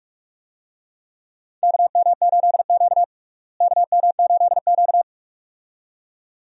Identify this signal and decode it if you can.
Morse code, heard as a keyed tone, about 46 wpm, 700 Hz. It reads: KM8Q KM8Q
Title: USA Call Signs - 4 Characters - Repeated 2x - 45wpm